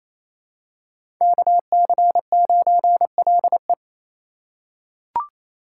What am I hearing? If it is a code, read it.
Morse code, heard as a keyed tone, about 28 wpm, 700 Hz. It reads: KC9LE
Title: USA Call Signs - 5 Characters - 2x Character Spacing - 28wpm